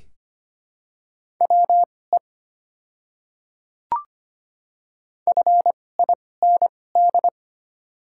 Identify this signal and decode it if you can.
Morse code, heard as a keyed tone, about 25 wpm, spaced as for 18 wpm, 700 Hz. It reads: WE FIND
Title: 2x Character Spacing - Top 200 Words - 25wpm